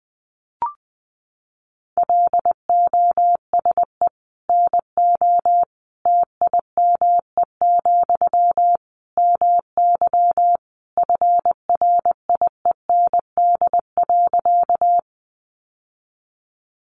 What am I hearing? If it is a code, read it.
Morse code, heard as a keyed tone, about 20 wpm, 700 Hz. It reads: LOSE NO TIME, MY FRIEND.
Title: Sentences from Top 1100 Words 20wpm